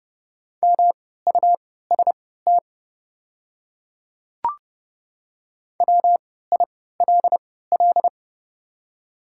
Morse code, heard as a keyed tone, about 30 wpm, spaced as for 17 wpm, 700 Hz. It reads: MUST WILL
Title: 3x Character Spacing - Top 200 Words - 30wpm